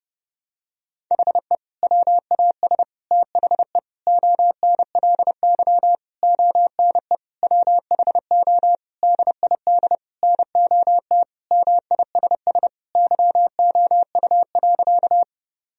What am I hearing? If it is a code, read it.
Morse code, heard as a keyed tone, about 30 wpm, 700 Hz. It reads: HE WAS THE ONLY ONE WHO DID NOT MISS YOU.
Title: Sentences from Top 1000 Words - Review Part 3 - 30wpm